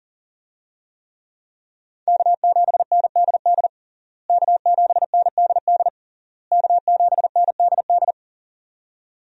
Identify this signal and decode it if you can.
Morse code, heard as a keyed tone, about 40 wpm, 700 Hz. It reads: K7NDD K7NDD K7NDD
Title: USA Call Signs - 5 Characters - Repeated 3x - 40wpm